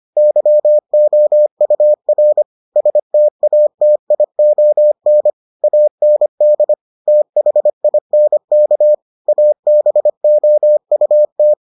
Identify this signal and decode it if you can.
Morse code, heard as a keyed tone, about 25 wpm, 600 Hz. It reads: YOUR STATION AND THINK ABOUT